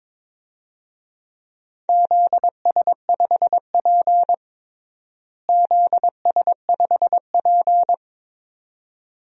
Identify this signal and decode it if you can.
Morse code, heard as a keyed tone, about 22 wpm, 700 Hz. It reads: ZS5P ZS5P
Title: DXCC-100 Call Signs - 4 Characters - Repeated 2x - 22wpm